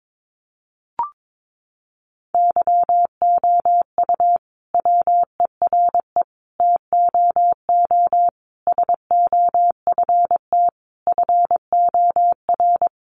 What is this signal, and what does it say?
Morse code, heard as a keyed tone, about 22 wpm, 700 Hz. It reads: YOU WERE TOO SOFT FOR
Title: Sentences from Top 1500 Words 22wpm